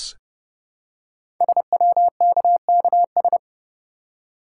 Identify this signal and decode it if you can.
Morse code, heard as a keyed tone, about 30 wpm, 700 Hz. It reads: SWKKS
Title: Code Groups - 5 Letters - 30wpm